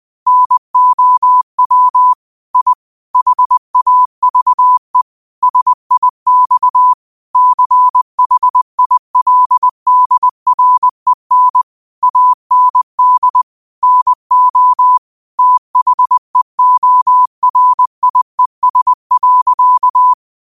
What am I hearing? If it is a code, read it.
Morse code, heard as a keyed tone, about 20 wpm, 1000 Hz. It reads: NOW I HAVE SIX CHILDREN AND NO THEORIES.